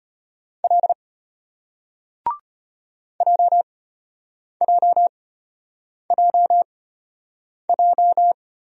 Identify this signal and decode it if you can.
Morse code, heard as a keyed tone, about 33 wpm, spaced as for 8 wpm, 700 Hz. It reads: L JJJJ